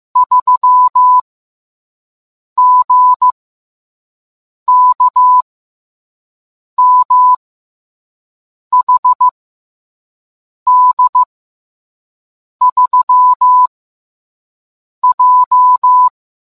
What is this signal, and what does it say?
Morse code, heard as a keyed tone, about 15 wpm, 1000 Hz. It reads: 3 G K M H D 3 J